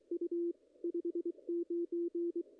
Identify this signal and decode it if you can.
Morse code, heard as a keyed tone, about 22 wpm, 340 Hz. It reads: U 59
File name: contests / ss-cw-k5tr-op-n5rz -0000